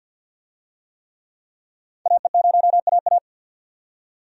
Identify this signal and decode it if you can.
Morse code, heard as a keyed tone, about 50 wpm, 700 Hz. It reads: AE0AA
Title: USA Call Signs - 5 Characters - 50wpm